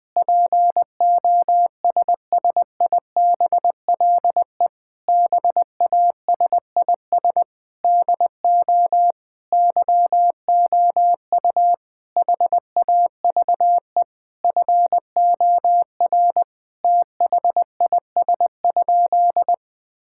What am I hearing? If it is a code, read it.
Morse code, heard as a keyed tone, about 20 wpm, 700 Hz. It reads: POSSIBLE BASIS DO YOU HAVE FOR THIS?
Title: Sentences from Top 1000 Words - Review Part 5 - 20wpm